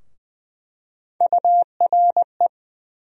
Morse code, heard as a keyed tone, about 20 wpm, 700 Hz. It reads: URE
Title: USA Call Sign Postfixes - 3 Letters 20wpm